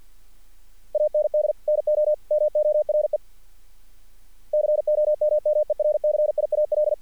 Morse code, heard as a keyed tone, about 50 wpm, 600 Hz. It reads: AND NO MORE COMMERCIAL
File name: archived / CRSnet